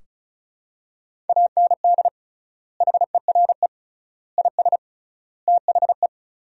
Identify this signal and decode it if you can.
Morse code, heard as a keyed tone, about 36 wpm, 700 Hz. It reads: AND HERE IS THE
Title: 3x Word-Spacing - Sentences from Top 500 Words - 35wpm